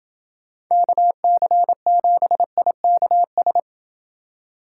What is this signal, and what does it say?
Morse code, heard as a keyed tone, about 28 wpm, 700 Hz. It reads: KC7IKS